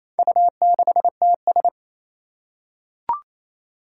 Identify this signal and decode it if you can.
Morse code, heard as a keyed tone, about 27 wpm, 700 Hz. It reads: U6TS
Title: Four Challenging Letter-Number Combinations 28wpm